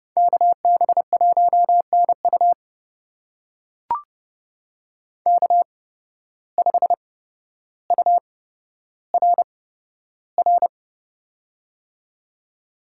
Morse code, heard as a keyed tone, about 28 wpm, spaced as for 13 wpm, 700 Hz. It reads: KB1NU K5URR
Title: USA Call Signs - 5 Characters - 8x Character Spacing - 30wpm